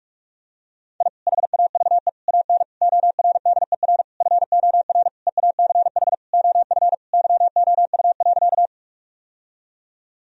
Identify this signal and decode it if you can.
Morse code, heard as a keyed tone, about 47 wpm, 700 Hz. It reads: I HAVE AN ORDER FOR EACH OF YOU.
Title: Sentences from Top 1000 Words - Review Part 2 - 45wpm